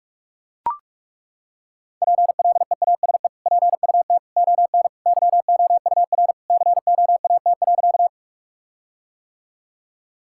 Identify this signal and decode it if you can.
Morse code, heard as a keyed tone, about 41 wpm, 700 Hz. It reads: PLEASE PUT ON YOUR COAT.
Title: Sentences from Top 1600 Words 45wpm